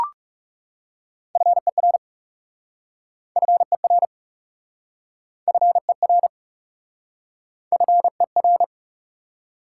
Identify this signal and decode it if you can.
Morse code, heard as a keyed tone, about 40 wpm, 700 Hz. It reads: FER FER FER FER